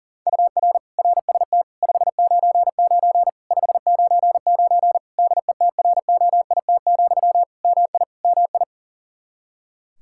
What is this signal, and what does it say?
Morse code, heard as a keyed tone, about 40 wpm, 700 Hz. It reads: UR RST 599 599 DETROIT, MI MI
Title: Practice QSO - Exchange 1 - Beginning - 40wpm